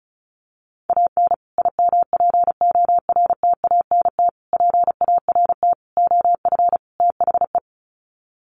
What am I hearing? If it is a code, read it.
Morse code, heard as a keyed tone, about 34 wpm, 700 Hz. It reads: AN IMPORTANT PART OF THE
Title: Sets of 5 Words Encore 35wpm